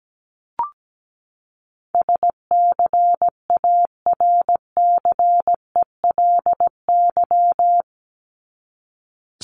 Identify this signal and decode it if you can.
Morse code, heard as a keyed tone, about 17 wpm, 700 Hz. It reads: SCARCELY
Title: LY Suffix - Top 100 Words with Base Words 17wpm